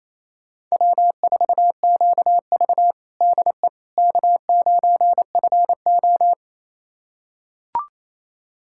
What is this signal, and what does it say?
Morse code, heard as a keyed tone, about 28 wpm, 700 Hz. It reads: W4QV DE K9FO
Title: USA Callsign DE Callsign - 4 Characters - 28wpm